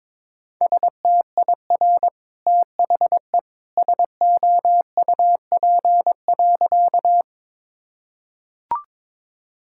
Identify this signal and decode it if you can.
Morse code, heard as a keyed tone, about 22 wpm, 690 Hz. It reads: STIR THE SOUP.
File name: practice / 22wpm / Easy Sentences with Hints - 22wpm